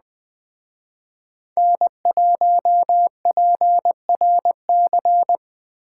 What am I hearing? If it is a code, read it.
Morse code, heard as a keyed tone, about 20 wpm, 700 Hz. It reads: N1PRC
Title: Speed Racing - USA Call Signs 17wpm